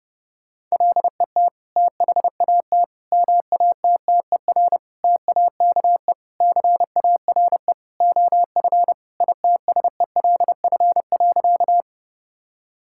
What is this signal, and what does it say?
Morse code, heard as a keyed tone, about 30 wpm, 700 Hz. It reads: LET THAT MATTER TAKE CARE OF ITSELF.
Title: Sentences from Top 1000 Words - Review Part 5 - 30wpm